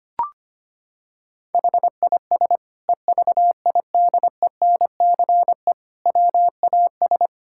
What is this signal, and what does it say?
Morse code, heard as a keyed tone, about 25 wpm, 700 Hz. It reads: HIS EVIDENCE WAS
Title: Sentences from Top 1600 Words 25wpm